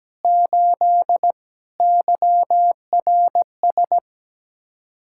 Morse code, heard as a keyed tone, about 17 wpm, 700 Hz. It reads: 8 YRS
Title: Practice QSO - Exchange 3 - More Info - 17wpm